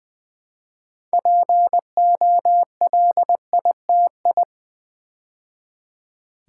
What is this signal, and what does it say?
Morse code, heard as a keyed tone, about 20 wpm, 700 Hz. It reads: POLITI
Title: Build A Word - 3 Letters At A Time - 9-Letter Words - 20wpm